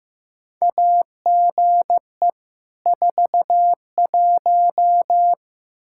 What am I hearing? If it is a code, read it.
Morse code, heard as a keyed tone, about 15 wpm, 700 Hz. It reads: AGE 41
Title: K1USN SST QSO Practice 15wpm